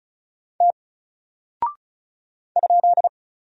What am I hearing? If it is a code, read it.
Morse code, heard as a keyed tone, about 35 wpm, 700 Hz. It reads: T ?
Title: Single Character (Letter:Number:Punctuation:Prosign) 35wpm